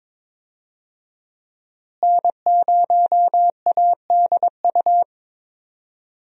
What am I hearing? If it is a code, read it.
Morse code, heard as a keyed tone, about 22 wpm, 700 Hz. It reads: N0ADU
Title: USA Call Signs - 5 Characters - Repeated 2x - 22wpm